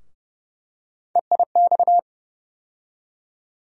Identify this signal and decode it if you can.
Morse code, heard as a keyed tone, about 30 wpm, 700 Hz. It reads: EIX
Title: USA Call Sign Postfixes - 3 Letters 30wpm